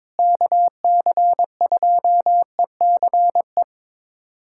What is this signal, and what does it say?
Morse code, heard as a keyed tone, about 22 wpm, 700 Hz. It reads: KC2ECE
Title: USA Callsign DE Callsign 22wpm